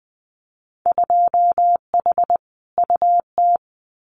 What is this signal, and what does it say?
Morse code, heard as a keyed tone, about 20 wpm, 700 Hz. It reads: 2H UT
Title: Winter Field Day Exchanges Encore 20wpm